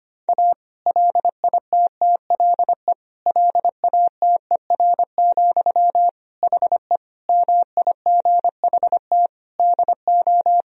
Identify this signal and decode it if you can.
Morse code, heard as a keyed tone, about 25 wpm, 700 Hz. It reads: A LITTLE LATER, HE MIGHT DO
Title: Sentences from Top 1000 Words - Review Part 4 - 25wpm